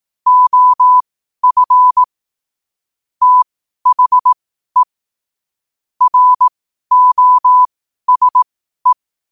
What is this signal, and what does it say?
Morse code, heard as a keyed tone, about 18 wpm, spaced as for 12 wpm, 1000 Hz. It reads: OF THE ROSE
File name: qotdfiles / 12wpm